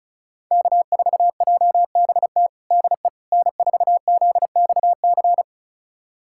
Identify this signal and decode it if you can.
Morse code, heard as a keyed tone, about 36 wpm, 700 Hz. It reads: K4JBT DE N4ZXC
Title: USA Callsign DE Callsign - 5 Characters - 35wpm